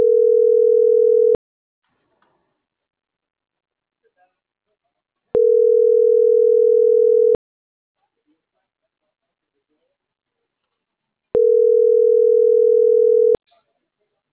Ringback tone (North America), 440 + 480 Hz.